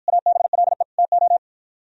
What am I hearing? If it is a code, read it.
Morse code, heard as a keyed tone, about 52 wpm, 700 Hz. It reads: ABLE TO